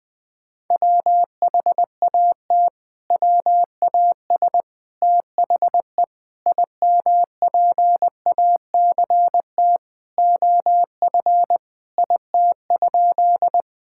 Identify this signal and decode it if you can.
Morse code, heard as a keyed tone, about 20 wpm, 700 Hz. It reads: WHAT WAS THE IMPACT OF IT?
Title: Sentences from Top 1000 Words - Review Part 6 - 20wpm